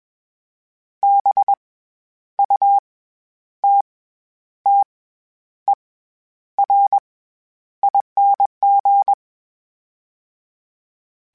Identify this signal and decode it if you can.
Morse code, heard as a keyed tone, about 21 wpm, 800 Hz. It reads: B U T T E R ING
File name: Special INGs-mp3s / Special INGs-mp3s / SlowToFaster ing @20